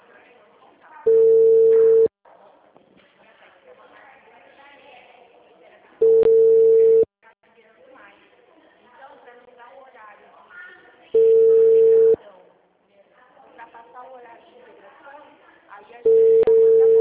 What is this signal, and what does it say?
Ringback tone (China, 450 Hz), with speech.
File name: out- -2007